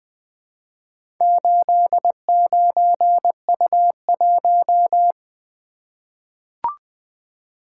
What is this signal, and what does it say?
Morse code, heard as a keyed tone, about 20 wpm, 700 Hz. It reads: 89U1
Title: Four Letter-Number Combinations - 20wpm